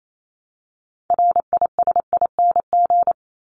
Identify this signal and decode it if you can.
Morse code, heard as a keyed tone, about 28 wpm, 700 Hz. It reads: RISING
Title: ING Suffix - Top 100 Words with Base Words 28wpm